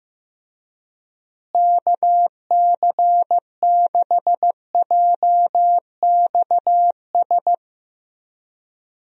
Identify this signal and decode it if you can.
Morse code, heard as a keyed tone, about 15 wpm, 700 Hz. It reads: KC6JXS
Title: USA Call Signs - 6 Characters - Repeated 2x - 15wpm